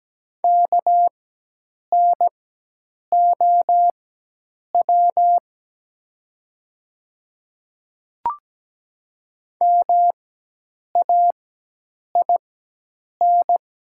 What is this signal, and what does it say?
Morse code, heard as a keyed tone, about 17 wpm, spaced as for 8 wpm, 700 Hz. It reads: KNOW MAIN
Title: 4x Character Spacing - Top 300 Words - 17wpm